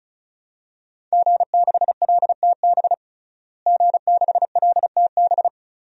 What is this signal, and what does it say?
Morse code, heard as a keyed tone, about 36 wpm, 700 Hz. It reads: G6LTB G6LTB
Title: DXCC-100 Call Signs - 5 Characters - Repeated 2x - 35wpm